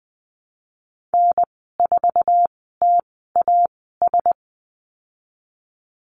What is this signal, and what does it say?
Morse code, heard as a keyed tone, about 19 wpm, spaced as for 14 wpm, 700 Hz. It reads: N4TAS
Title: 048 Lesson 9g - Number 4 - Call Signs 20wpm